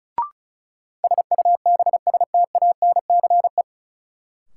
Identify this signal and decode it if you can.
Morse code, heard as a keyed tone, about 34 wpm, 700 Hz. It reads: SUBSTANCE
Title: Sentences from Top 1500 Words - New Word Drill - 35wpm